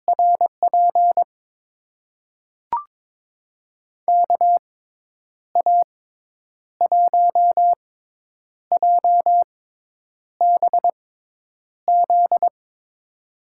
Morse code, heard as a keyed tone, about 21 wpm, spaced as for 8 wpm, 700 Hz. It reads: RP KA1JBZ
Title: USA Call Signs - 6 Characters - 6x Character Spacing - 22wpm